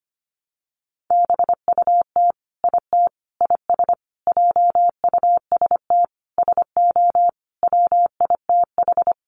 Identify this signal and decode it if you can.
Morse code, heard as a keyed tone, about 25 wpm, 700 Hz. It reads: BUT IT IS JUST SO WITH
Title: Sentences from Top 200 Words 25wpm